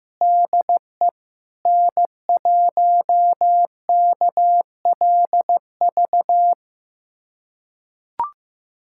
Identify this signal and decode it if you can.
Morse code, heard as a keyed tone, about 16 wpm, 700 Hz. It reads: DE N1KLV
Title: USA Callsign DE Callsign 15wpm